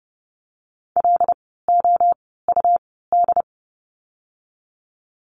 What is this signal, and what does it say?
Morse code, heard as a keyed tone, about 30 wpm, spaced as for 17 wpm, 700 Hz. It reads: LOUD
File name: Lesson 17f - Letter u - Words 30wpm